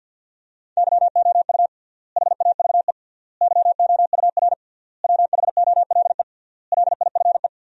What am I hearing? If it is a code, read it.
Morse code, heard as a keyed tone, about 52 wpm, 700 Hz. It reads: YOU HAVE YOUR WHOLE LIFE
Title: 3x Word-Spacing - Sentences from Top 400 Words - 50wpm